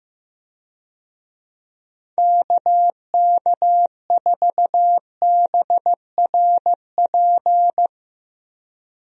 Morse code, heard as a keyed tone, about 15 wpm, 700 Hz. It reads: KK4BRP
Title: USA Call Signs - 6 Characters - Repeated 2x - 15wpm